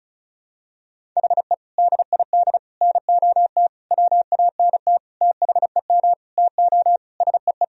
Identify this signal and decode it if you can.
Morse code, heard as a keyed tone, about 34 wpm, 700 Hz. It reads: HE DID NOT WANT THEM TO SEE
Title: Sentences from Top 1000 Words - Review Part 1 - 35wpm